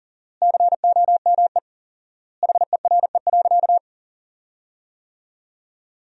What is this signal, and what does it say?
Morse code, heard as a keyed tone, about 40 wpm, 700 Hz. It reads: COME HERE.
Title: 4x Word-Spacing - Sentences from Top 300 Words - 40wpm